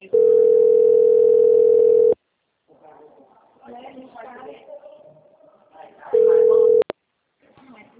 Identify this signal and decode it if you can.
Ringback tone (North America), 440 + 480 Hz, with speech.
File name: out- - - - - .20314